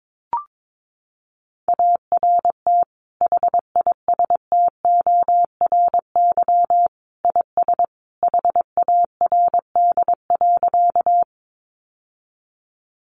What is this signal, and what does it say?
Morse code, heard as a keyed tone, about 22 wpm, 700 Hz. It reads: ART HISTORY IS HARD.
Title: Sentences from Top 600 Words 22wpm